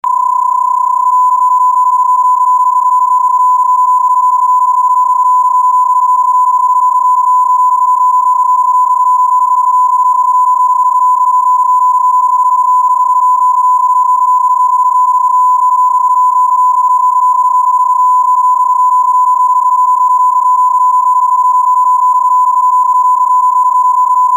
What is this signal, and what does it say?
PSK31, about 31.25 baud.